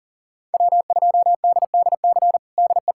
Morse code, heard as a keyed tone, about 40 wpm, 700 Hz. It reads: W2DDC DE
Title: USA Callsign DE Callsign - 5 Characters - 40wpm